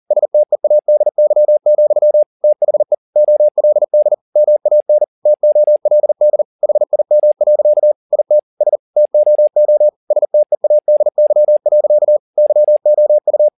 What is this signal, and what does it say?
Morse code, heard as a keyed tone, about 40 wpm, 600 Hz. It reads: STEADY, THE OLD MAN TOLD HIM. IT S TOO STEADY. YOU